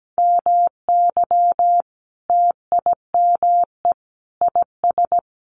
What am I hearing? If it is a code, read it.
Morse code, heard as a keyed tone, about 17 wpm, 700 Hz. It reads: MY TIME IS
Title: Sentences from Top 1100 Words 17wpm